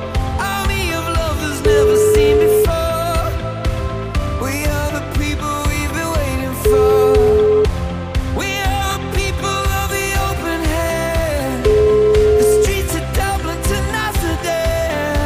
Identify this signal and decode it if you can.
Ringback tone (Europe, 425 Hz), with noise.